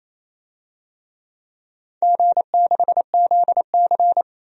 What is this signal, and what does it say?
Morse code, heard as a keyed tone, about 28 wpm, 700 Hz. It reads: G6ZC
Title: DXCC-100 Call Signs - 4 Characters - Repeated 2x - 28wpm